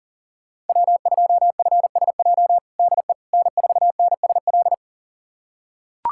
Morse code, heard as a keyed tone, about 40 wpm, 700 Hz. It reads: W2FSJ DE N4NSL
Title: USA Callsign DE Callsign - 5 Characters - 40wpm